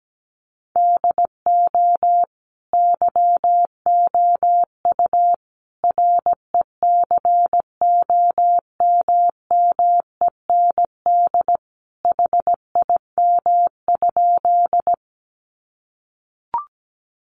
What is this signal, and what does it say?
Morse code, heard as a keyed tone, about 17 wpm, 700 Hz. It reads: DO YOU RECOMMEND HIM?
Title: Sentences from Top 1200 Words 17wpm